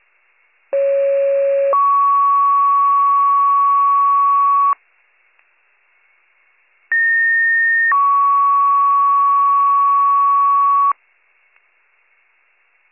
Two-tone page, heard as a one-second tone, then a three-second tone, about 550 then 1120 Hz, about 1850 then 1120 Hz. It is heard from 0.7 to 10.9 s.